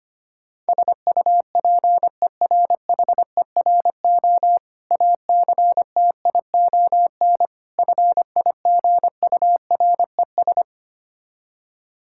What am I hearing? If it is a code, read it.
Morse code, heard as a keyed tone, about 25 wpm, 700 Hz. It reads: SUPERHERO ACTION FIGURES
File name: practice / 25wpm / Difficult Words with Hints - 25wpm